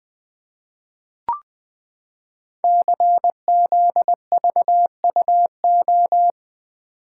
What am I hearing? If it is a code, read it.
Morse code, heard as a keyed tone, about 20 wpm, 700 Hz. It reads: CZVUO
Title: Code Groups - 5 Letter:Numbers - 20wpm